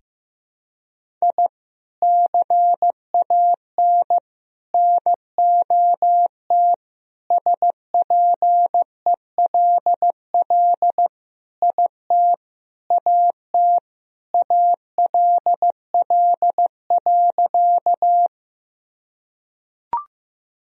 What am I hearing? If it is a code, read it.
Morse code, heard as a keyed tone, about 15 wpm, 700 Hz. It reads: I CAN NOT SPELL IT AT ALL.
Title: Sentences from Top 1000 Words - Review Part 1 - 15wpm